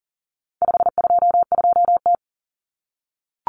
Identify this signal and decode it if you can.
Morse code, heard as a keyed tone, about 40 wpm, 700 Hz. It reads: 522T
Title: Common Cut Numbers - 4-Digit 40wpm